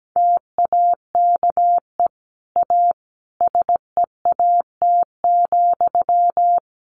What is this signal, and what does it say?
Morse code, heard as a keyed tone, about 17 wpm, 700 Hz. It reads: TAKE A SEAT,
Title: Sentences from Top 1500 Words 17wpm